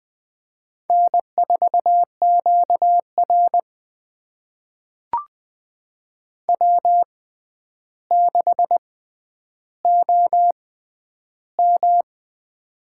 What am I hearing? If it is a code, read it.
Morse code, heard as a keyed tone, about 20 wpm, spaced as for 10 wpm, 700 Hz. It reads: N4QR W6OM
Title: USA Call Signs - 4 Characters - 6x Character Spacing - 20wpm